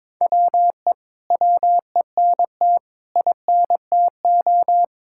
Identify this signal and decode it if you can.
Morse code, heard as a keyed tone, about 21 wpm, 700 Hz. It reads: WE WENT INTO
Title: Sentences from Top 1600 Words 22wpm